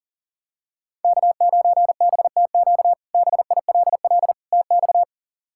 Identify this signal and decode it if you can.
Morse code, heard as a keyed tone, about 40 wpm, 700 Hz. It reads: K9BTQ BILL TX
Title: K1USN SST QSO Practice - 40wpm